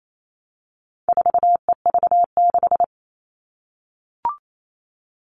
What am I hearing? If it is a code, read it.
Morse code, heard as a keyed tone, about 28 wpm, 700 Hz. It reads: 4EV6
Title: All Cut Numbers - 4-Digit 28wpm